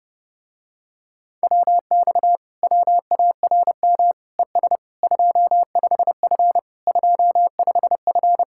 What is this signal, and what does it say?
Morse code, heard as a keyed tone, about 31 wpm, 700 Hz. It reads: WX WARM ES 25F 25F